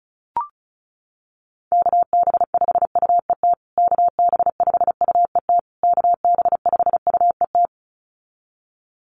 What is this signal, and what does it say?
Morse code, heard as a keyed tone, about 34 wpm, 700 Hz. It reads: KB5UET KB5UET KB5UET
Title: USA Call Signs Repeated 3x 35wpm